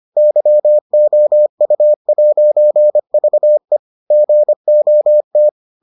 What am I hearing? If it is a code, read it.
Morse code, heard as a keyed tone, about 25 wpm, 600 Hz. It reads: YOU'VE GOT